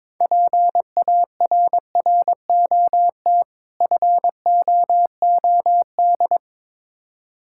Morse code, heard as a keyed tone, about 22 wpm, 700 Hz. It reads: PARROT FOOD
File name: Difficult Words with Hints - 22wpm